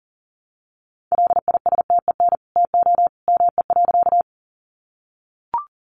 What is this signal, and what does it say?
Morse code, heard as a keyed tone, about 40 wpm, 700 Hz. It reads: LISTEN TO ME.
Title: Sentences from Top 400 Words 40wpm